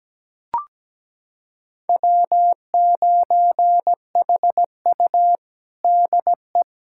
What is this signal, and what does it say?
Morse code, heard as a keyed tone, about 17 wpm, 700 Hz. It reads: W9HU DE
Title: USA Callsign DE Callsign - 4 Characters - 17wpm